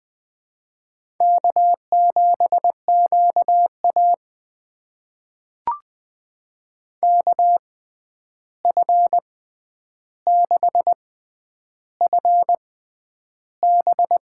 Morse code, heard as a keyed tone, about 20 wpm, spaced as for 10 wpm, 700 Hz. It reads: K7QA KF6FB
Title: USA Call Signs - 6x Character Spacing - 20wpm